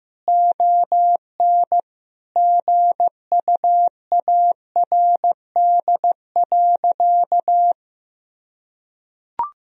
Morse code, heard as a keyed tone, about 15 wpm, 700 Hz. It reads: ON GUARD.
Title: Sentences from Top 1000 Words - Review Part 4 - 15wpm